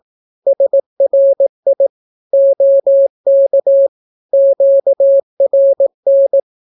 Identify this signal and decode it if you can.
Morse code, heard as a keyed tone, about 18 wpm, 550 Hz. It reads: SRI OK QRN
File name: naukapodstaw / skroty / skroty